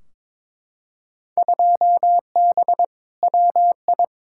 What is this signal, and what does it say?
Morse code, heard as a keyed tone, about 21 wpm, 700 Hz. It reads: 2B WI
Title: Field Day Exchanges - 22wpm